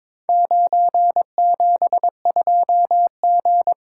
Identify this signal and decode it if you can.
Morse code, heard as a keyed tone, about 22 wpm, 700 Hz. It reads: 972G
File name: practice / 22wpm / Four Letter-Number Combinations - 22wpm